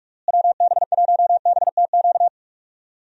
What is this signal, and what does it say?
Morse code, heard as a keyed tone, about 46 wpm, 700 Hz. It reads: WB1BTQ
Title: USA Call Signs - 6 Characters - Repeated 3x - 45wpm